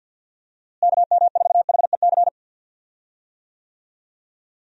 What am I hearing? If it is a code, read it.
Morse code, heard as a keyed tone, about 49 wpm, 700 Hz. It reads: KM4HEC
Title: USA Call Signs - 6 Characters - Repeated 2x - 50wpm